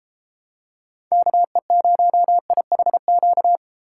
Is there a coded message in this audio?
Yes: Morse code from 1.1 to 3.6 s, heard as a keyed tone, about 32 wpm, 700 Hz. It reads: KE0IHQ